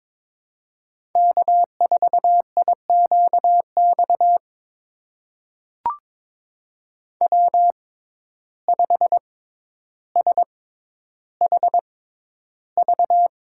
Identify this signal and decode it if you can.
Morse code, heard as a keyed tone, about 22 wpm, spaced as for 11 wpm, 700 Hz. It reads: K4IQX W5SHV
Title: USA Call Signs - 6x Character Spacing - 22wpm